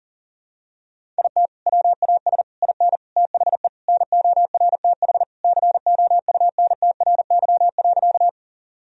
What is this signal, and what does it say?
Morse code, heard as a keyed tone, about 37 wpm, 700 Hz. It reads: IT WAS IN THE NORTH COUNTRY.
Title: Sentences from Top 1000 Words - Review Part 1 - 40wpm